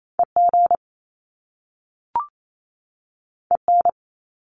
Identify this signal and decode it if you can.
Morse code, heard as a keyed tone, about 28 wpm, 700 Hz. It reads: EG EN
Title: Top Two Letter Combinations in Words 28wpm